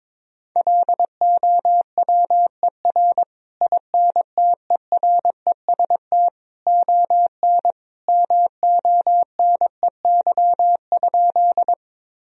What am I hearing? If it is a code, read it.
Morse code, heard as a keyed tone, about 22 wpm, 700 Hz. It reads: LOWER INTEREST ON MONEY?
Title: Sentences from Top 1000 Words - Review Part 4 - 22wpm